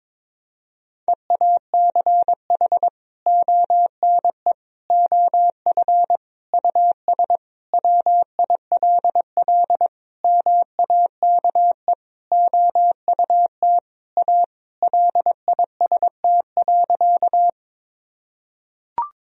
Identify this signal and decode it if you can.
Morse code, heard as a keyed tone, about 22 wpm, 700 Hz. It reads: EACH ONE OF US WILL MAKE OUT A LIST.
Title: Sentences from Top 1000 Words - Review Part 2 - 22wpm